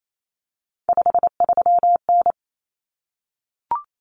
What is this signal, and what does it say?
Morse code, heard as a keyed tone, about 28 wpm, 700 Hz. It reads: 53N
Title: Common Cut Numbers - 3-Digit 28wpm